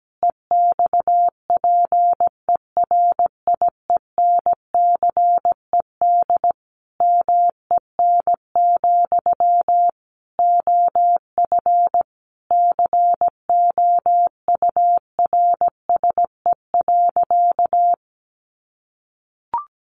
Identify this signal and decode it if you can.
Morse code, heard as a keyed tone, about 17 wpm, 700 Hz. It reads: EXPERIENCED MEN, OF COURSE.
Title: Sentences from Top 1100 Words 17wpm